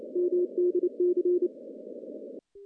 Morse code, heard as a keyed tone, about 28 wpm, 360 Hz. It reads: MDC
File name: contests / CW-SS-k5tr- -1500